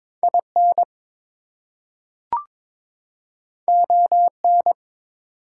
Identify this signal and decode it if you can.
Morse code, heard as a keyed tone, about 22 wpm, 700 Hz. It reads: IN ON
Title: Two Letter Words - 22wpm